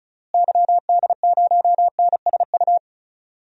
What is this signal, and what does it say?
Morse code, heard as a keyed tone, about 36 wpm, 700 Hz. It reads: YD0NSU
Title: DXCC-100 Call Signs - 6 Characters - Repeated 3x - 35wpm